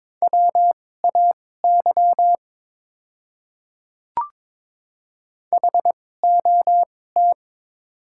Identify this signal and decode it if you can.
Morse code, heard as a keyed tone, about 21 wpm, spaced as for 16 wpm, 700 Hz. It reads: WAY HOT
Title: 2x Character Spacing - Top 100 Words - 22wpm